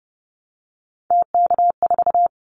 Morse code, heard as a keyed tone, about 30 wpm, 700 Hz. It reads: TK4
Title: Callsign Prefixes for Top 100 DXCC - 30wpm